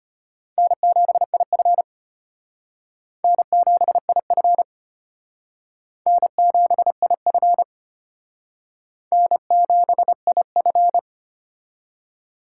Morse code, heard as a keyed tone, about 34 wpm, 700 Hz. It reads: N7IF N7IF N7IF N7IF